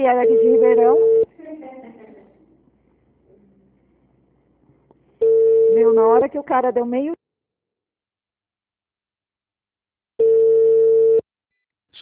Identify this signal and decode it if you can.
Ringback tone (China, 450 Hz), with speech.